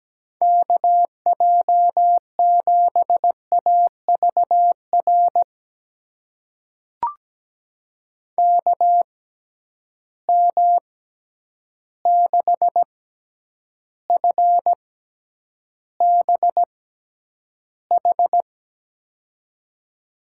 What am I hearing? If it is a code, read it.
Morse code, heard as a keyed tone, about 17 wpm, spaced as for 9 wpm, 700 Hz. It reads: KJ7AVR KM6FBH
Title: USA Call Signs - 6 Characters - 6x Character Spacing - 17wpm